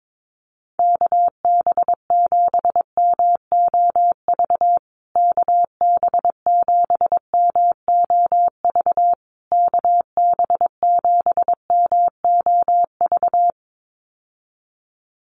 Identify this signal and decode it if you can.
Morse code, heard as a keyed tone, about 22 wpm, 700 Hz. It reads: KB7MOV KB7MOV KB7MOV
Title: USA Call Signs Repeated 3x 22wpm